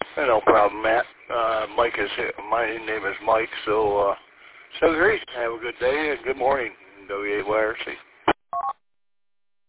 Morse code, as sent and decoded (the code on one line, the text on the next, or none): none